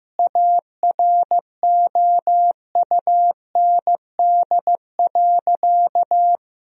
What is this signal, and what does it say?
Morse code, heard as a keyed tone, about 15 wpm, 700 Hz. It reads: AROUND.
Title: Sentences from Top 1000 Words - Review Part 4 - 15wpm